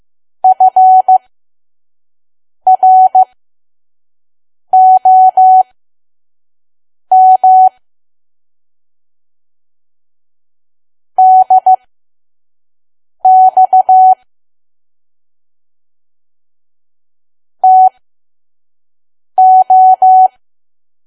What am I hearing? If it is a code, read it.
Morse code, heard as a keyed tone, about 15 wpm, spaced as for 5 wpm, 750 Hz. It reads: FROM DX TO
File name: Archive / 5 WPM files / PM